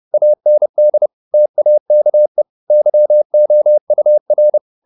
Morse code, heard as a keyed tone, about 30 wpm, 600 Hz. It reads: AND TAKE YOUR
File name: the old man and the sea / the old man and the sea 1 30wpm